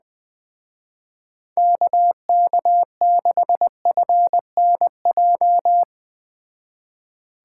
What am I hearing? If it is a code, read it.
Morse code, heard as a keyed tone, about 20 wpm, 700 Hz. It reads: KK6FNJ
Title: Speed Racing - USA 2x3 Call Signs 17wpm